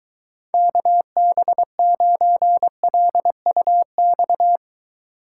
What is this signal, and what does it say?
Morse code, heard as a keyed tone, about 23 wpm, 700 Hz. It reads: KB9LUX